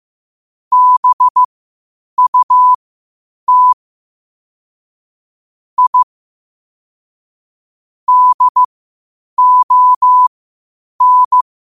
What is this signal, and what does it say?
Morse code, heard as a keyed tone, about 15 wpm, spaced as for 8 wpm, 1000 Hz. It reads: BUT I DON
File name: qotdfiles / 8wpm